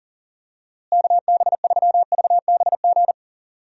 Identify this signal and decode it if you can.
Morse code, heard as a keyed tone, about 40 wpm, 700 Hz. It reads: KB3VBG
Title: USA Call Signs - 6 Characters - Repeated 3x - 40wpm